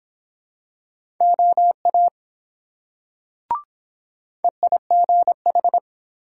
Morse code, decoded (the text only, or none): OA EIGH